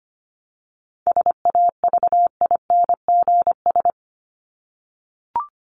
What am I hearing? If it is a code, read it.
Morse code, heard as a keyed tone, about 25 wpm, 700 Hz. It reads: SAVINGS
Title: Top 1001-1250 Words 25wpm